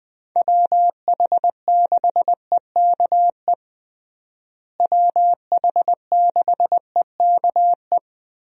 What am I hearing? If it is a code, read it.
Morse code, heard as a keyed tone, about 21 wpm, 700 Hz. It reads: WH6EKE WH6EKE
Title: USA Call Signs - 6 Characters - Repeated 2x - 20wpm